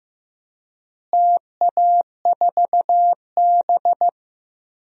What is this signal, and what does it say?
Morse code, heard as a keyed tone, about 15 wpm, 700 Hz. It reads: TA4B
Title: DXCC-100 Call Signs - 4 Characters - Repeated 2x - 15wpm